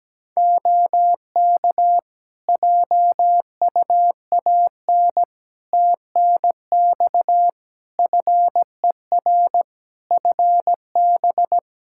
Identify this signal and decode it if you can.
Morse code, heard as a keyed tone, about 17 wpm, 700 Hz. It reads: OK JUAN TNX FER FB
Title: Practice QSO - Exchange 4 - Ending - 17wpm